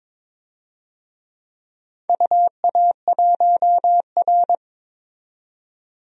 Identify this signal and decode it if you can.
Morse code, heard as a keyed tone, about 22 wpm, 700 Hz. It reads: UA1R
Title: DXCC-100 Call Signs - 4 Characters - 22wpm